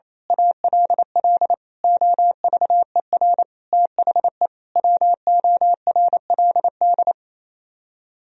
Morse code, heard as a keyed tone, about 26 wpm, 700 Hz. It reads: ALL OVER THE WORLD
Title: Speed-Racing - Sets of 5 Words 28wpm